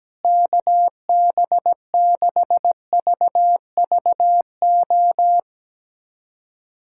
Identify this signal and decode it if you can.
Morse code, heard as a keyed tone, about 17 wpm, 700 Hz. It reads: KB6VVO
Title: USA Callsign DE Callsign 17wpm